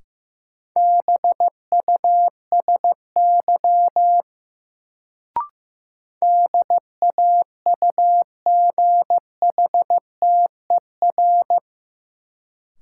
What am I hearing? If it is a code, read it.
Morse code, heard as a keyed tone, about 15 wpm, 700 Hz. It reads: BUSY DAUGHTER
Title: Sentences from Top 1500 Words - New Word Drill Alt - 15wpm